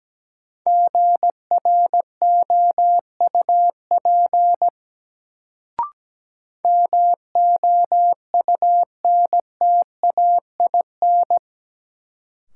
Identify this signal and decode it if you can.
Morse code, heard as a keyed tone, about 17 wpm, 700 Hz. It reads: GROUP MOUNTAIN
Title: Sentences from Top 1000 Words - Review Part 1 - New Word Drill - 17wpm